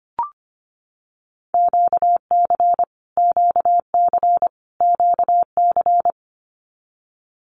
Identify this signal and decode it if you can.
Morse code, heard as a keyed tone, about 25 wpm, 700 Hz. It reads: QC QC QC
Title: Canadian Provinces and Territories - Repeated 3x - 25wpm